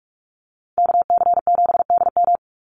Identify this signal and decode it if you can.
Morse code, heard as a keyed tone, about 44 wpm, 700 Hz. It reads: KC7DM
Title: NA SOTA Activator Callsigns 45wpm